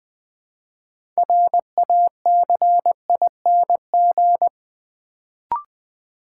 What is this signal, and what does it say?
Morse code, heard as a keyed tone, about 20 wpm, 700 Hz. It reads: RACING
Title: Build A Word - 2 Letters At A Time - 6-Letter Words - 20wpm